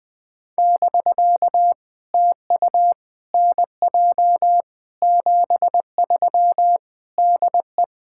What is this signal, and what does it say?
Morse code, heard as a keyed tone, about 19 wpm, 700 Hz. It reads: <BK> TU NJ 73 DE
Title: POTA QSO Practice - 20wpm